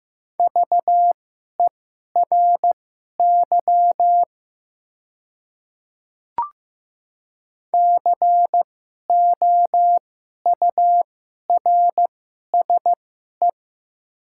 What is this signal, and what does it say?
Morse code, heard as a keyed tone, about 15 wpm, spaced as for 11 wpm, 700 Hz. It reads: VERY COURSE
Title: 2x Character Spacing - Top 500 Words - 15wpm